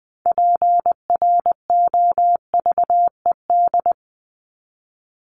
Morse code, heard as a keyed tone, about 20 wpm, 700 Hz. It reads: PROVED
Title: ED Suffix - Top 100 Words with Base Words 20wpm